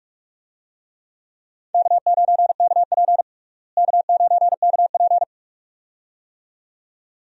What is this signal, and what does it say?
Morse code, heard as a keyed tone, about 43 wpm, 700 Hz. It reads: K9KP K9KP
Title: USA Call Signs - 4 Characters - Repeated 2x - 45wpm